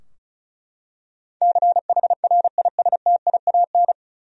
Morse code, heard as a keyed tone, about 36 wpm, 700 Hz. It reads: CHRISTIAN
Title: USA Common Names - 35wpm